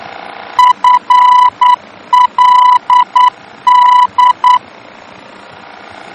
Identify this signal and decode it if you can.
Morse code, heard as a keyed tone, about 9 wpm, 990 Hz. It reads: FLD
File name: germany / mx / FLD-D-VOR